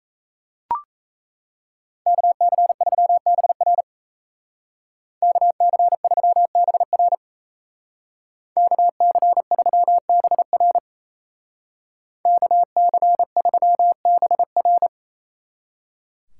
Morse code, heard as a keyed tone, about 36 wpm, 700 Hz. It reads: KC3BR KC3BR KC3BR KC3BR